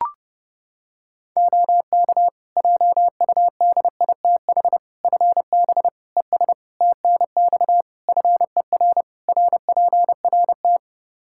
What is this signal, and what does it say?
Morse code, heard as a keyed tone, about 28 wpm, 700 Hz. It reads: OK JUDITH FB ES TNX FER RPRT
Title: Practice QSO - Exchange 2 - Info - 30wpm